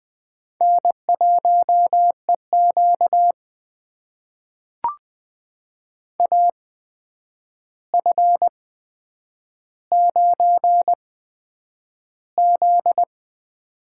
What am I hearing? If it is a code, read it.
Morse code, heard as a keyed tone, about 20 wpm, spaced as for 9 wpm, 700 Hz. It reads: N1EQ AF9Z
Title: USA Call Signs - 4 Characters - 8x Character Spacing - 20wpm